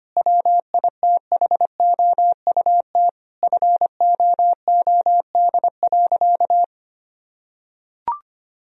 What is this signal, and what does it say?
Morse code, heard as a keyed tone, about 25 wpm, 700 Hz. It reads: WITHOUT FOOD.
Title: Sentences from Top 1000 Words - Review Part 3 - 25wpm